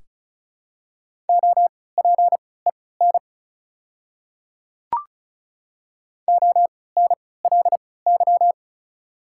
Morse code, heard as a keyed tone, about 36 wpm, spaced as for 20 wpm, 700 Hz. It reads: OPEN ONLY
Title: 3x Character Spacing - Top 300 Words - 35wpm